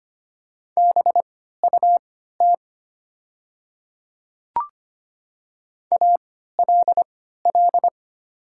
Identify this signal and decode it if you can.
Morse code, heard as a keyed tone, about 25 wpm, spaced as for 14 wpm, 690 Hz. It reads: BUT ALL
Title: 3x Character Spacing - Top 200 Words - 25wpm